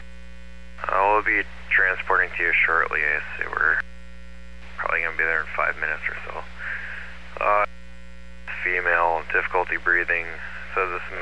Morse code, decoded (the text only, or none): none